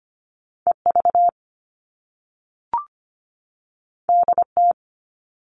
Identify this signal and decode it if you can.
Morse code, heard as a keyed tone, about 25 wpm, 700 Hz. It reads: EV DT
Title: All Cut Numbers - 2-Digit 25wpm